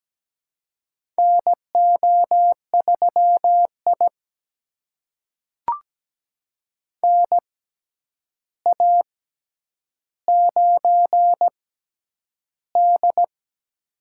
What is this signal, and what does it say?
Morse code, heard as a keyed tone, about 17 wpm, spaced as for 9 wpm, 700 Hz. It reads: NO3I NA9D
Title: USA Call Signs - 4 Characters - 6x Character Spacing - 17wpm